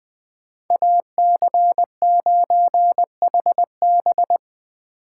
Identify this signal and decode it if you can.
Morse code, heard as a keyed tone, about 24 wpm, 700 Hz. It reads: AC9HB
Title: Speed Racing - USA 2x2 Call Signs 20wpm